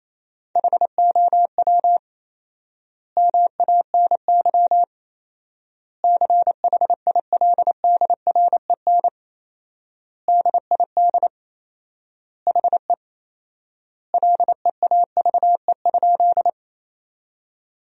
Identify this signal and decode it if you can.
Morse code, heard as a keyed tone, about 28 wpm, 700 Hz. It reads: HOW MANY CHILDREN DID HE LEAVE?
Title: 4x Word-Spacing - Sentences from Top 400 Words - 28wpm